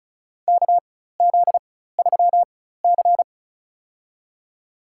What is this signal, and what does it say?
Morse code, heard as a keyed tone, about 34 wpm, spaced as for 16 wpm, 700 Hz. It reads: KZ3C
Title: USA Call Signs - 4 Characters - 4x Character Spacing - 35wpm